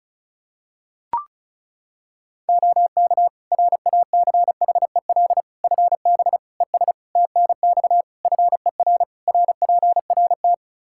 Morse code, heard as a keyed tone, about 36 wpm, 700 Hz. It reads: OK RACHEL FB ES TNX FER RPRT
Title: Practice QSO - Exchange 2 - Info - 35wpm